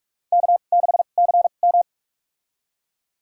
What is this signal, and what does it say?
Morse code, heard as a keyed tone, about 42 wpm, spaced as for 31 wpm, 700 Hz. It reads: K6CM
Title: USA Call Signs - 4 Characters - 2x Character Spacing - 45wpm